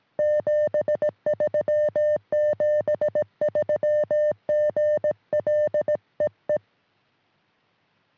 Morse code, heard as a keyed tone, about 18 wpm, 590 Hz. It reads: 73 73 GL EE
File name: sm7lcb / 1940 ur5lx 23cm